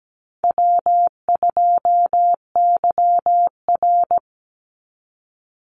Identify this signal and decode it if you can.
Morse code, heard as a keyed tone, about 17 wpm, 700 Hz. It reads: W2YR
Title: Easy Call Signs Repeated 3x - 17wpm